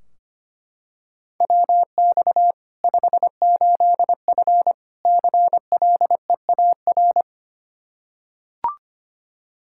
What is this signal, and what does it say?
Morse code, heard as a keyed tone, about 25 wpm, 700 Hz. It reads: WX 58F CLEAR
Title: Practice QSO - Exchange 2 - Info - 25wpm